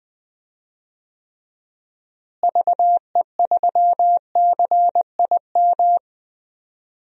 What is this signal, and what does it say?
Morse code, heard as a keyed tone, about 20 wpm, 700 Hz. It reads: VE3CIM
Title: DXCC-100 Call Signs - 6 Characters - Repeated 2x - 20wpm